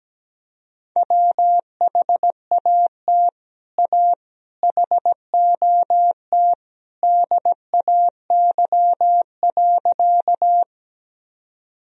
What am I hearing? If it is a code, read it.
Morse code, heard as a keyed tone, about 17 wpm, 700 Hz. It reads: WHAT A HOT DAY.
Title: Sentences from Top 1000 Words - Review Part 3 - 17wpm